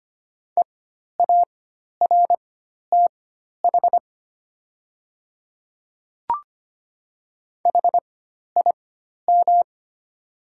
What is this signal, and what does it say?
Morse code, heard as a keyed tone, about 25 wpm, spaced as for 12 wpm, 700 Hz. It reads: EARTH HIM
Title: 4x Character Spacing - Top 300 Words - 25wpm